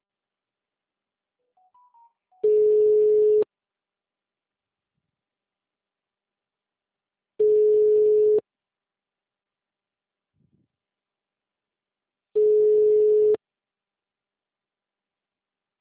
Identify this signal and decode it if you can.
Ringback tone (Europe, 425 Hz).
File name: out- -2001- - - .5